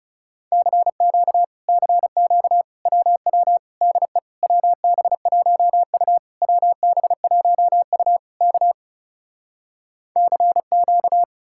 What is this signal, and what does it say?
Morse code, heard as a keyed tone, about 38 wpm, 700 Hz. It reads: CQ CQ WW DE WB1U WB1U K CQ
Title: Speed-Racing Common QSO Phrases - 30wpm